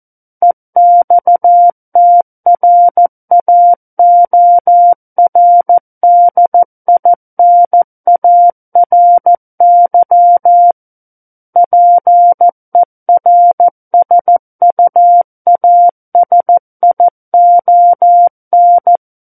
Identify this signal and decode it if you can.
Morse code, heard as a keyed tone, about 14 wpm, 700 Hz. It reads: EXTRAORDINARY PERSUASION
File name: Book One / 05-Chapter 5 - 14 WPM